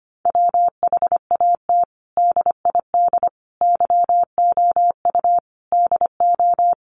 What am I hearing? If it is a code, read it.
Morse code, heard as a keyed tone, about 25 wpm, 700 Hz. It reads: WHAT DID YOU DO
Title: Sentences from Top 100 Words 25wpm